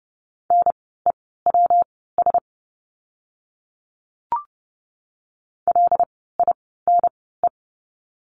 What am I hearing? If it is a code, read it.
Morse code, heard as a keyed tone, about 30 wpm, spaced as for 17 wpm, 700 Hz. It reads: NEWS LINE
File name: course / 30:10wpm / 115 Lesson 19f - Letter w - Words 30wpm